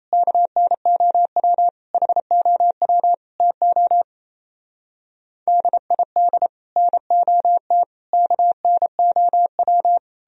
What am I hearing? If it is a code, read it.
Morse code, heard as a keyed tone, about 35 wpm, 700 Hz. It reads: KNOW HOW TO DID NOT KNOW